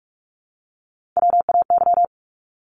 Morse code, heard as a keyed tone, about 44 wpm, 700 Hz. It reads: WAY
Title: Top 500 Words 45wpm